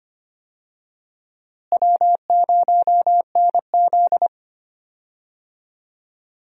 Morse code, heard as a keyed tone, about 25 wpm, 700 Hz. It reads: W0NZ
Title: USA Call Signs - 4 Characters - Repeated 3x - 25wpm